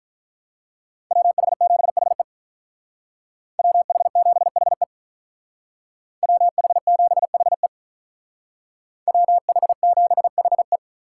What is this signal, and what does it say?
Morse code, heard as a keyed tone, about 43 wpm, 700 Hz. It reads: WH7HE WH7HE WH7HE WH7HE
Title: Speed Racing - USA 2x2 Call Signs 35wpm